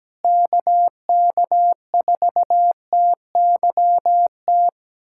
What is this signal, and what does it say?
Morse code, heard as a keyed tone, about 17 wpm, 700 Hz. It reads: KK4TYT
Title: USA Callsign DE Callsign - 6 Characters - 17wpm